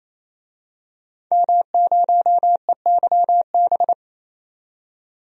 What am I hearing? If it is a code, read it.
Morse code, heard as a keyed tone, about 28 wpm, 700 Hz. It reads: M0EYB
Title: DXCC-100 Call Signs - 5 Characters - 28wpm